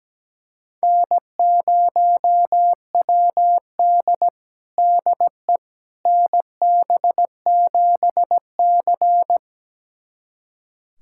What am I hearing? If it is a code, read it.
Morse code, heard as a keyed tone, about 17 wpm, 700 Hz. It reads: N0WD DE NB7C
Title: USA Callsign DE Callsign - 4 Characters - 17wpm